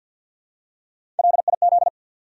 Morse code, heard as a keyed tone, about 53 wpm, 700 Hz. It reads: LIZ
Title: USA Common Names - Max 4 Letters - 50wpm